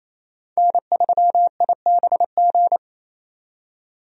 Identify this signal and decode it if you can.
Morse code, heard as a keyed tone, about 28 wpm, 700 Hz. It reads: N3IBG
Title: USA Call Signs - 5 Characters - 28wpm